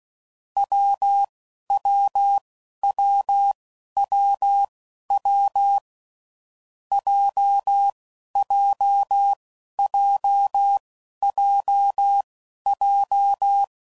Morse code, heard as a keyed tone, about 16 wpm, spaced as for 11 wpm, 800 Hz. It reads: WWWWW JJJJJ
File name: Combinations / Difficult Combos / WPJ WPJ1